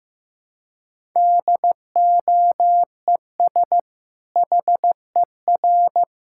Morse code, heard as a keyed tone, about 16 wpm, 700 Hz. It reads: DOES HER
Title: Sentences from Top 1000 Words - Review Part 5 - 15wpm